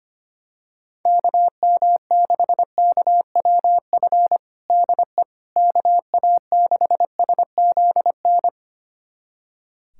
Morse code, heard as a keyed tone, about 25 wpm, 700 Hz. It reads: KM6KWF DE KA6SZN
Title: USA Callsign DE Callsign - 6 Characters - 25wpm